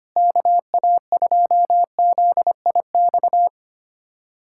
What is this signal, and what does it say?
Morse code, heard as a keyed tone, about 25 wpm, 700 Hz. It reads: KA2ZIX
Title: USA Callsign DE Callsign 25wpm